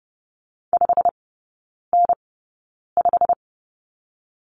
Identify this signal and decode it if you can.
Morse code, heard as a keyed tone, about 30 wpm, spaced as for 9 wpm, 700 Hz. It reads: 5N5
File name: Lesson 16e - Letter c - Three Characters 30wpm